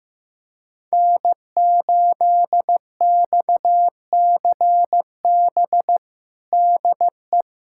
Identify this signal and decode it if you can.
Morse code, heard as a keyed tone, about 15 wpm, 700 Hz. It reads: N8XCB DE
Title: USA Callsign DE Callsign - 5 Characters - 15wpm